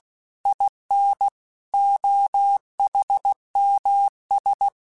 Morse code, heard as a keyed tone, about 16 wpm, 800 Hz. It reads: IN OHMS
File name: Technical Subjects / Technical Subjects-mp3s / XL @15